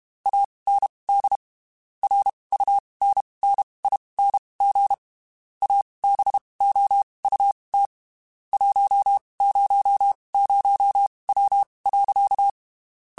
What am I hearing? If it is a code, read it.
Morse code, heard as a keyed tone, about 32 wpm, spaced as for 23 wpm, 800 Hz. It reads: AND RUNNING ABOUT 100W.
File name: (spaced) / QSOs Spaced-mp3s / WD0FBT 1 @30 sp